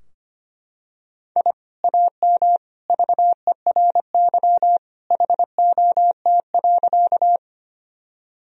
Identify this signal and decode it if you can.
Morse code, heard as a keyed tone, about 25 wpm, 700 Hz. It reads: I AM VERY HOT.
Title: Sentences from Top 1000 Words - Review Part 3 - 25wpm